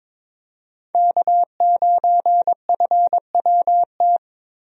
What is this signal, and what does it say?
Morse code, heard as a keyed tone, about 22 wpm, 700 Hz. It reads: K9FWT
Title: USA Call Signs - 5 Characters - Repeated 3x - 22wpm